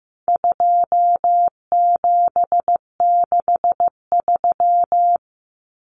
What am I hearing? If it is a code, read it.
Morse code, heard as a keyed tone, about 15 wpm, 700 Hz. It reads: 2763
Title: Common Cut Numbers - 4-Digit 15wpm